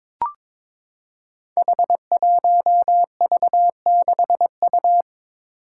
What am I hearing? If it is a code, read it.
Morse code, heard as a keyed tone, about 22 wpm, 700 Hz. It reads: H1V6U
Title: Five Challenging Letter-Number Combinations 22wpm